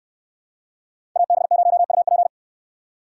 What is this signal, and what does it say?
Morse code, heard as a keyed tone, about 67 wpm, 700 Hz. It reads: AB0UP